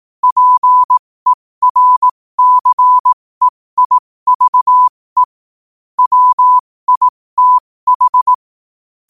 Morse code, heard as a keyed tone, about 18 wpm, spaced as for 15 wpm, 1000 Hz. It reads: PERCEIVE WITH
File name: qotdfiles / 15wpm